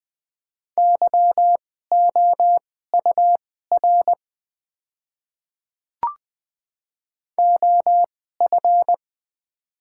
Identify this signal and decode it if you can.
Morse code, heard as a keyed tone, about 20 wpm, spaced as for 14 wpm, 700 Hz. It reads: YOUR OF
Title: 2x Character Spacing - Top 100 Words - 20wpm